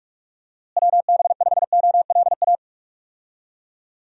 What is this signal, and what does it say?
Morse code, heard as a keyed tone, about 47 wpm, 700 Hz. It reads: WB5OLA
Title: USA Call Signs - 6 Characters - Repeated 2x - 45wpm